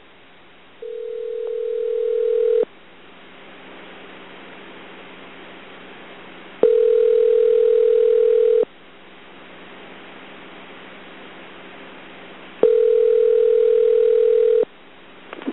Ringback tone (North America), 440 + 480 Hz.